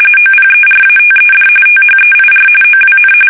RTTY, about 45.45 baud. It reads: OPERATED THEIR OWN
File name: fsk850 / usnavy1